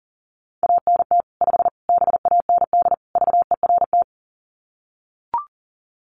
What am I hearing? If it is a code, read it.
Morse code, heard as a keyed tone, about 40 wpm, 700 Hz. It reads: ANT 5 BAND VERT
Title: Common QSO Phrases 40wpm